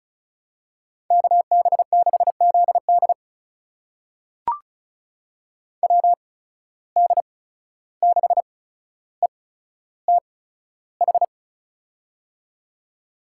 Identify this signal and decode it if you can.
Morse code, heard as a keyed tone, about 36 wpm, spaced as for 14 wpm, 700 Hz. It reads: KB6ZD WD6ETH
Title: USA Call Signs - 8x Character Spacing - 35wpm